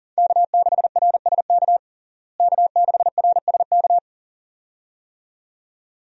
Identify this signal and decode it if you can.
Morse code, heard as a keyed tone, about 40 wpm, 700 Hz. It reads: K6RSK K6RSK
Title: USA Call Signs - 5 Characters - Repeated 2x - 40wpm